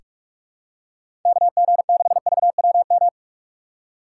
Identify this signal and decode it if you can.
Morse code, heard as a keyed tone, about 44 wpm, 700 Hz. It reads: KG6VWM
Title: USA Call Signs - 3x Character Spacing - 45wpm